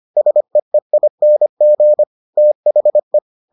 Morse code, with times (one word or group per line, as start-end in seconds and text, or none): 0.2-2.0 SEEING
2.4-3.2 THE